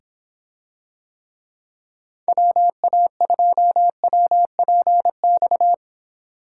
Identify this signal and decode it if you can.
Morse code, heard as a keyed tone, about 26 wpm, 700 Hz. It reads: WA2WPX